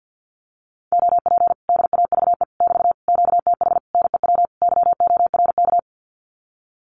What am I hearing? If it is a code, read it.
Morse code, heard as a keyed tone, about 50 wpm, 700 Hz. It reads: OP DAVE <BT> QTH NEW YORK
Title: Common QSO Phrases Encore 50wpm